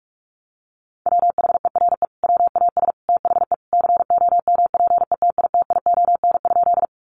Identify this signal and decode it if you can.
Morse code, heard as a keyed tone, about 44 wpm, 700 Hz. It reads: WHERE WAS THE COMPETITION?
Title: Sentences from Top 1000 Words 45wpm